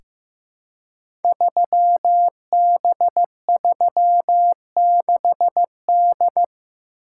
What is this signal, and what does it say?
Morse code, heard as a keyed tone, about 15 wpm, 700 Hz. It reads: 3B36D
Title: Code Groups - 5 Letter:Numbers - 15wpm